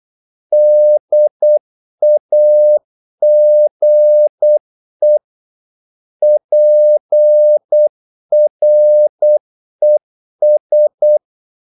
Morse code, heard as a keyed tone, about 8 wpm, 600 Hz. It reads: DAGE PRES
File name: Korte ord DK 8wpm